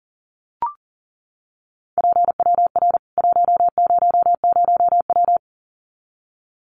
Morse code, heard as a keyed tone, about 40 wpm, 700 Hz. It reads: PWR 100W
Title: Common QSO Phrases 40wpm